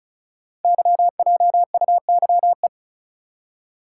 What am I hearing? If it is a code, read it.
Morse code, heard as a keyed tone, about 35 wpm, 690 Hz. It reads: YJUYE
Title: Code Groups - 5 Letters - 35wpm